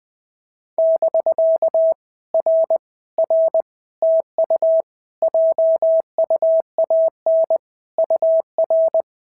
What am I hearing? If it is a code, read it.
Morse code, heard as a keyed tone, about 20 wpm, 650 Hz. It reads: <BK> R R TU JUAN UR
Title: POTA QSO Practice - 20wpm